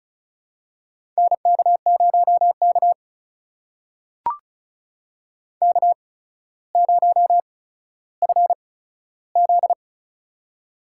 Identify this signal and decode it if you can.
Morse code, heard as a keyed tone, about 34 wpm, spaced as for 15 wpm, 700 Hz. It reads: NK0K K0FZ
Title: USA Call Signs - 8x Character Spacing - 35wpm